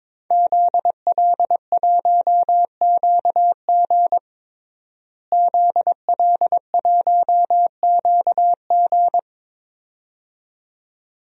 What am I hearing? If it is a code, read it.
Morse code, heard as a keyed tone, about 22 wpm, 700 Hz. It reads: ZL1QG ZL1QG
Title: DXCC-100 Call Signs - 5 Characters - Repeated 2x - 22wpm